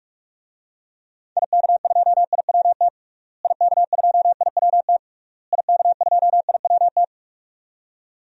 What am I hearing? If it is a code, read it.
Morse code, heard as a keyed tone, about 46 wpm, 700 Hz. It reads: IK2IWT IK2IWT IK2IWT
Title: DXCC-100 Call Signs - 6 Characters - Repeated 3x - 45wpm